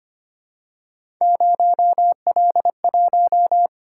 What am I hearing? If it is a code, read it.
Morse code, heard as a keyed tone, about 26 wpm, 700 Hz. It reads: 0L1
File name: Three Letter-Number Combinations - 25wpm